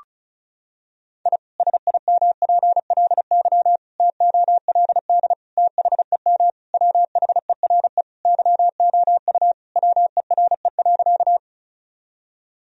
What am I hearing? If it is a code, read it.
Morse code, heard as a keyed tone, about 36 wpm, 700 Hz. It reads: I SIMPLY TOLD THEM WHERE YOU WERE.
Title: Sentences from Top 1000 Words - Review Part 4 - 35wpm